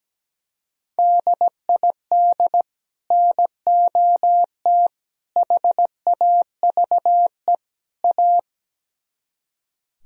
Morse code, heard as a keyed tone, about 17 wpm, 700 Hz. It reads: DID NOT HAVE A
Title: Speed-Racing - Sets of 4 Words 17wpm